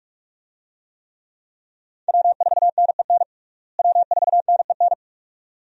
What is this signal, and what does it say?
Morse code, heard as a keyed tone, about 45 wpm, 700 Hz. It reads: W4NEN W4NEN
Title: USA Call Signs - 5 Characters - Repeated 2x - 45wpm